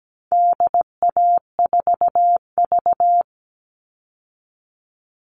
Morse code, heard as a keyed tone, about 17 wpm, 700 Hz. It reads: DA4V
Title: All Cut Numbers - 4-Digit 17wpm